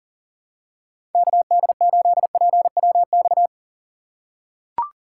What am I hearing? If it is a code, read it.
Morse code, heard as a keyed tone, about 40 wpm, 700 Hz. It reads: KD8PWX
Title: USA Call Signs - 3x Character Spacing - 40wpm